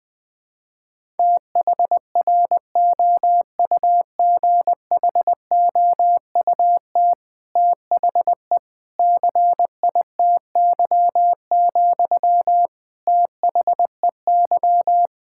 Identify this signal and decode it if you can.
Morse code, heard as a keyed tone, about 20 wpm, 700 Hz. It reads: THROUGHOUT THE CITY, THEY
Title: Sentences from Top 1000 Words - Review Part 5 - 20wpm